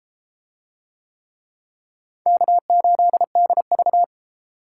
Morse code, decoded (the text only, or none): K8DV